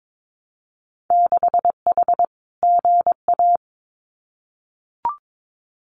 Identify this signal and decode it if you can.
Morse code, heard as a keyed tone, about 22 wpm, 700 Hz. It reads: 6H GA
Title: Winter Field Day Exchanges Encore 22wpm